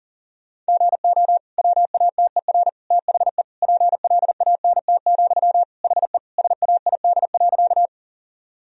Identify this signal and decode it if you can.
Morse code, heard as a keyed tone, about 37 wpm, 700 Hz. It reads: GO WATER THE PLANT, HE SAID.
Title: Sentences from Top 1000 Words - Review Part 1 - 40wpm